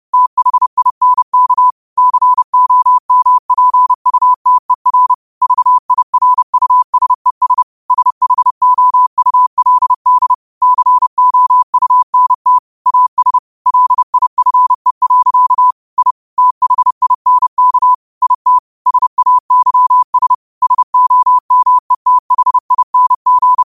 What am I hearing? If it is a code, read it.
Morse code, heard as a keyed tone, about 30 wpm, 1000 Hz. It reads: THINK COMPUTER VIRUSES SHOULD COUNT AS LIFE. I THINK IT SAYS SOMETHING